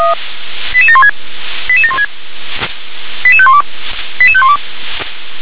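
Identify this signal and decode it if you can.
Five-tone selective call (ZVEI), heard as quick runs of five tones, tones of about 70 ms. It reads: 88716 88716 88512 88512
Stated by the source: ZVEI five-tone selective call received by SDR.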